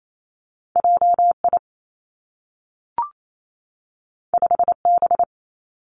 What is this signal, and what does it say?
Morse code, heard as a keyed tone, about 28 wpm, 700 Hz. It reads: J I 5 B
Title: Two Challenging Letter-Number Combinations 28wpm